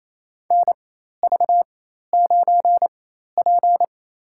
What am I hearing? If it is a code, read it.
Morse code, heard as a keyed tone, about 28 wpm, spaced as for 13 wpm, 700 Hz. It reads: NV9P
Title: USA Call Signs - 4x Character Spacing - 28wpm